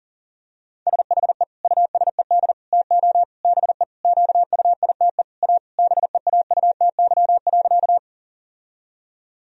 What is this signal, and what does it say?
Morse code, heard as a keyed tone, about 40 wpm, 700 Hz. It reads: SHE USED TO BE QUITE A BEAUTY.
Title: Sentences from Top 1000 Words - Review Part 2 - 40wpm